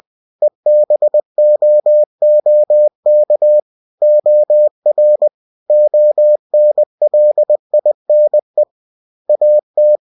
Morse code, heard as a keyed tone, about 20 wpm, 600 Hz. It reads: EBOOK OR ONLINE AT
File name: winnie-20wpm